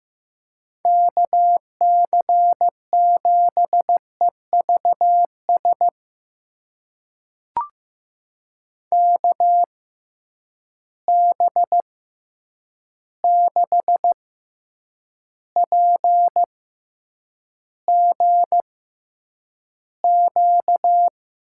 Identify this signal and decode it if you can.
Morse code, heard as a keyed tone, about 15 wpm, spaced as for 8 wpm, 700 Hz. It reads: KC7EVS KB6PGQ
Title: USA Call Signs - 6 Characters - 6x Character Spacing - 15wpm